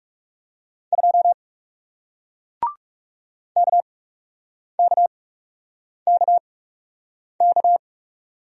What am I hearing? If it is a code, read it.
Morse code, heard as a keyed tone, about 40 wpm, spaced as for 8 wpm, 700 Hz. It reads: 2 KKKK